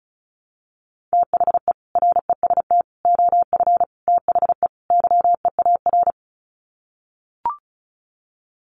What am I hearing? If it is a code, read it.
Morse code, heard as a keyed tone, about 34 wpm, 700 Hz. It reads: THE REST OF THE YEAR
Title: Sets of 5 Words Encore 35wpm